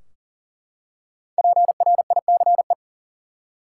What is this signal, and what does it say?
Morse code, heard as a keyed tone, about 40 wpm, 700 Hz. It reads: PRICE
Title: Five Letter Words - 40wpm